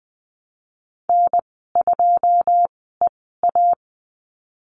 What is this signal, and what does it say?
Morse code, heard as a keyed tone, about 20 wpm, spaced as for 14 wpm, 700 Hz. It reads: N2EA
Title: 128 Lesson 21g - Number 6 - Call Signs 20wpm